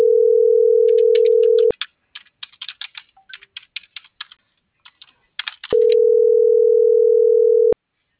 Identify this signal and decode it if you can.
Ringback tone (North America), 440 + 480 Hz.
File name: CENTRAL FRALSE / out fernanda.fralse - .96456